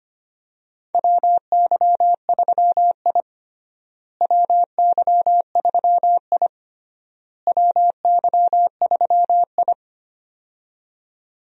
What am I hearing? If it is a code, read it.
Morse code, heard as a keyed tone, about 25 wpm, 700 Hz. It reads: WY3I WY3I WY3I
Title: USA Call Signs - 4 Characters - Repeated 3x - 25wpm